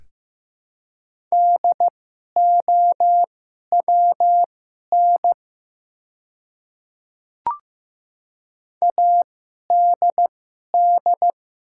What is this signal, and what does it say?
Morse code, heard as a keyed tone, about 15 wpm, spaced as for 11 wpm, 700 Hz. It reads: DOWN ADD
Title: 2x Character Spacing - Top 500 Words - 15wpm